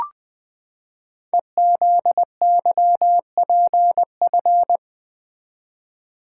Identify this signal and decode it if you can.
Morse code, heard as a keyed tone, about 20 wpm, 700 Hz. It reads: EZYPF
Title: Code Groups - 5 Letters - 20wpm